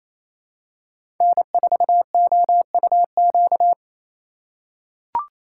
Morse code, heard as a keyed tone, about 28 wpm, 700 Hz. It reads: N4OUQ
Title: USA Call Signs - 5 Characters - 4x Character Spacing - 28wpm